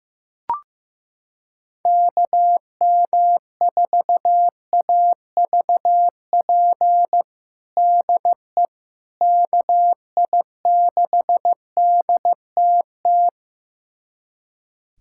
Morse code, heard as a keyed tone, about 15 wpm, 700 Hz. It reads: KM4AVP DE KI6DTT
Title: USA Callsign DE Callsign - 6 Characters - 15wpm